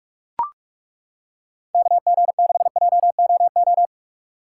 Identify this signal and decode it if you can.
Morse code, heard as a keyed tone, about 42 wpm, 700 Hz. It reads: KG6JOO